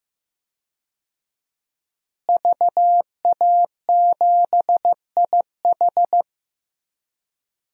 Morse code, heard as a keyed tone, about 15 wpm, 700 Hz. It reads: VA7IH
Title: DXCC-100 Call Signs - 5 Characters - Repeated 2x - 15wpm